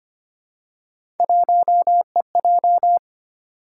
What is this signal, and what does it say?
Morse code, heard as a keyed tone, about 25 wpm, 700 Hz. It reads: 1EJ
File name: Three Letter-Number Combinations - 25wpm